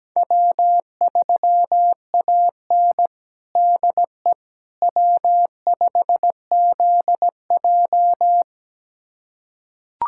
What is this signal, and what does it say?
Morse code, heard as a keyed tone, about 17 wpm, 700 Hz. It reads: W3AN DE W5ZJ
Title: USA Callsign DE Callsign - 4 Characters - 17wpm